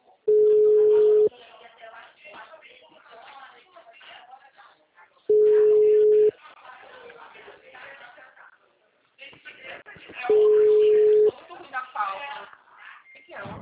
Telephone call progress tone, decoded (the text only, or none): ringback tone